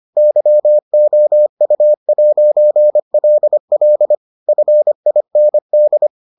Morse code, heard as a keyed tone, about 25 wpm, 600 Hz. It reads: YOU'LL FIND